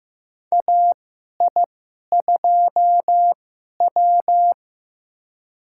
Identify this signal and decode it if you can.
Morse code, heard as a keyed tone, about 15 wpm, spaced as for 11 wpm, 700 Hz. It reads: AI2W
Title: USA Call Signs - 4 Characters - 2x Character Spacing - 15wpm